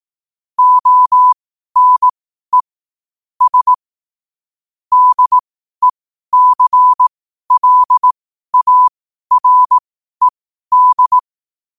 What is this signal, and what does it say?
Morse code, heard as a keyed tone, about 18 wpm, spaced as for 12 wpm, 1000 Hz. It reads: ONE S DECLARED
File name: qotdfiles / 12wpm